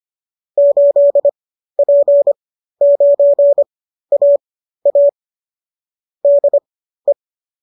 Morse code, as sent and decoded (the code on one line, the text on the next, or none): ---.. .--. ----. .- .- / -.. .
8P9AA DE